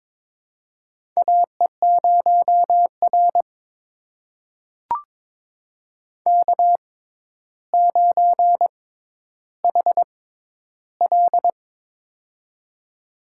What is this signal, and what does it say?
Morse code, heard as a keyed tone, about 22 wpm, spaced as for 11 wpm, 700 Hz. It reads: AE0R K9HL
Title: USA Call Signs - 4 Characters - 6x Character Spacing - 22wpm